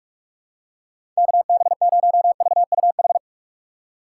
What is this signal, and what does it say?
Morse code, heard as a keyed tone, about 42 wpm, 700 Hz. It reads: KB0VUH
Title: USA Call Signs - 6 Characters - 3x Character Spacing - 45wpm